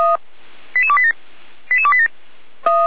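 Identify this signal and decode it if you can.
Five-tone selective call (ZVEI), heard as quick runs of five tones, tones of about 70 ms. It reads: 88287 88287
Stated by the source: ZVEI five-tone selective call 88335 received on an SDR.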